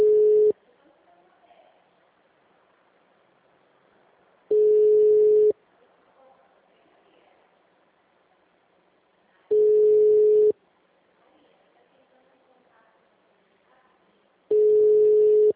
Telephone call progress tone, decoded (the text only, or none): ringback tone